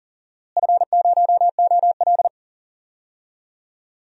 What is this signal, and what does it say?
Morse code, heard as a keyed tone, about 40 wpm, 700 Hz. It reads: F0OL
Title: DXCC-100 Call Signs - 4 Characters - Repeated 2x - 40wpm